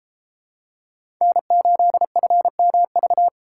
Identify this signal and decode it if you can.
Morse code, heard as a keyed tone, about 31 wpm, 700 Hz. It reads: N8FMV